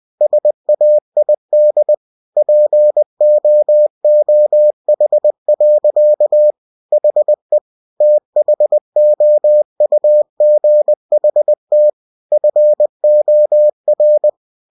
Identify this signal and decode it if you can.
Morse code, heard as a keyed tone, about 20 wpm, 600 Hz. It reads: SAID POOH. HE THOUGHT FOR
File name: cw-books / winnie-the-pooh / winnie-20wpm-page-0045